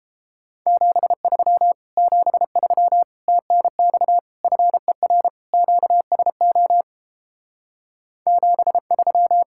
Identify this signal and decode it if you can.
Morse code, heard as a keyed tone, about 29 wpm, 700 Hz. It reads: 73 73 TNX FER QSO 73